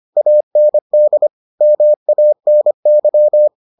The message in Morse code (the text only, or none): AND MANY